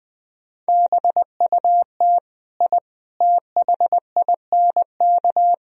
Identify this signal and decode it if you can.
Morse code, heard as a keyed tone, about 20 wpm, 700 Hz. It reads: BUT I THINK
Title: Speed-Racing - Sets of 4 Words 17wpm